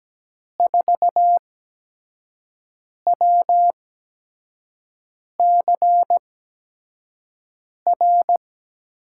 Morse code, heard as a keyed tone, about 17 wpm, spaced as for 5 wpm, 700 Hz. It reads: 4WCR
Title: USA Call Signs - 5 Characters - 8x Character Spacing - 17wpm